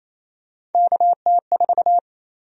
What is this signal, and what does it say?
Morse code, heard as a keyed tone, about 28 wpm, 700 Hz. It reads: KT4
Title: USA Call Sign Prefixes - 28wpm